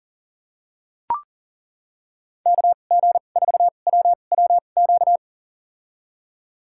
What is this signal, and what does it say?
Morse code, heard as a keyed tone, about 40 wpm, spaced as for 29 wpm, 700 Hz. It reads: KG4WWQ
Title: USA Call Signs - 6 Characters - 2x Character Spacing - 40wpm